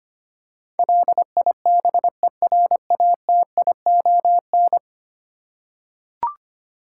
Morse code, heard as a keyed tone, about 25 wpm, 700 Hz. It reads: LIBERATION
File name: Ten Letter Words - 25wpm